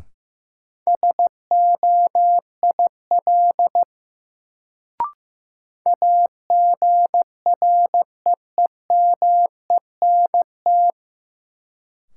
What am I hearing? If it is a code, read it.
Morse code, heard as a keyed tone, about 15 wpm, 700 Hz. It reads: SOIL AGREEMENT
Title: Sentences from Top 1500 Words - New Word Drill Alt - 15wpm